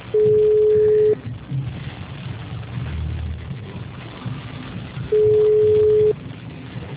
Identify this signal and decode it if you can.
Ringback tone (Europe, 425 Hz).